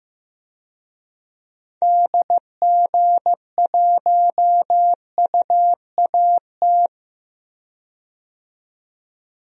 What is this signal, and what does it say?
Morse code, heard as a keyed tone, about 15 wpm, 690 Hz. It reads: DG1UAT
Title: DXCC-100 Call Signs - 6 Characters - 15wpm